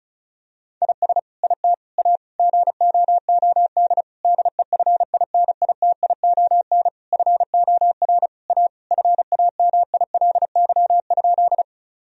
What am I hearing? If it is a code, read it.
Morse code, heard as a keyed tone, about 36 wpm, 700 Hz. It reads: IS IT A GOOD DEFINITION FOR A FAMILY?
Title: Sentences from Top 1000 Words - Review Part 4 - 35wpm